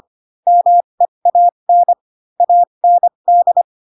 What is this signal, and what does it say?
Morse code, heard as a keyed tone, about 25 wpm, spaced as for 22 wpm, 700 Hz. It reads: MEAN AND
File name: Pinocchio / Pinocchio ch3 18 25 ik1hge